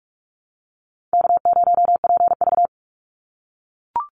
Morse code, heard as a keyed tone, about 44 wpm, 700 Hz. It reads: K0PV
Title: Call Signs - 45wpm